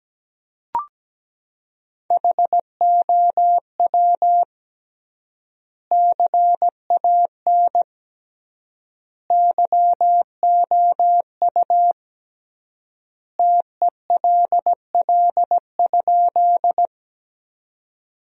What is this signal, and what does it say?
Morse code, heard as a keyed tone, about 17 wpm, 700 Hz. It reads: HOW CAN YOU TELL?
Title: 3x Word-Spacing - Sentences from Top 200 Words - 17wpm